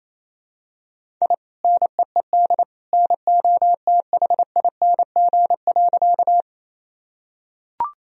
Morse code, heard as a keyed tone, about 28 wpm, 700 Hz. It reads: I NEED NOTHING.
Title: Sentences from Top 1000 Words - Review Part 1 - 28wpm